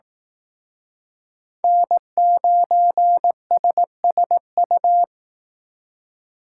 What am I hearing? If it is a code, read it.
Morse code, heard as a keyed tone, about 18 wpm, 700 Hz. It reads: N9SSU
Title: Speed Racing - USA 1x3 Call Signs 15wpm